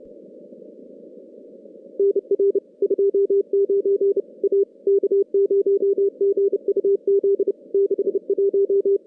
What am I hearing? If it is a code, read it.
Morse code, heard as a keyed tone, about 30 wpm, 400 Hz. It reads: NR 29 A K0GUZ 61
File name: contests / CW-SS-k5tr- -2100